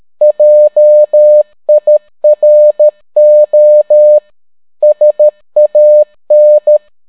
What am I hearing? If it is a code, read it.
Morse code, heard as a keyed tone, about 13 wpm, 600 Hz. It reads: JIRO SAN